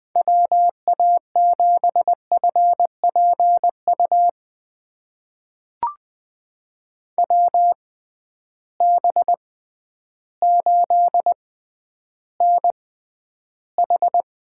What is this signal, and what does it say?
Morse code, heard as a keyed tone, about 20 wpm, spaced as for 11 wpm, 700 Hz. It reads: WA7FPU WB8NH
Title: USA Call Signs - 6 Characters - 6x Character Spacing - 20wpm